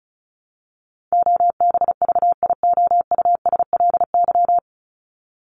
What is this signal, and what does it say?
Morse code, heard as a keyed tone, about 34 wpm, 700 Hz. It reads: OBVIOUSLY
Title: LY Suffix - Top 100 Words with Base Words 35wpm